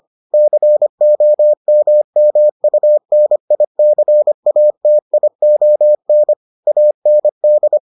Morse code, heard as a keyed tone, about 25 wpm, 600 Hz. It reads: COMMUNICATION AND